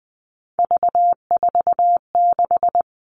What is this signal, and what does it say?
Morse code, heard as a keyed tone, about 20 wpm, 700 Hz. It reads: V46
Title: All Cut Numbers - 3-Digit 20wpm